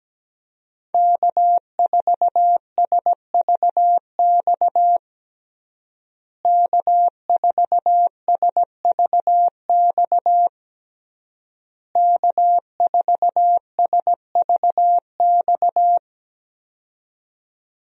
Morse code, heard as a keyed tone, about 17 wpm, 700 Hz. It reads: K4SVX K4SVX K4SVX
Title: USA Call Signs - 5 Characters - Repeated 3x - 17wpm